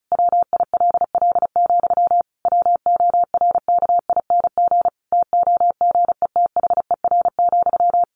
Morse code, heard as a keyed tone, about 34 wpm, 700 Hz. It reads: WILL, WORKING TOGETHER,
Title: Sentences from Top 1400 Words 35wpm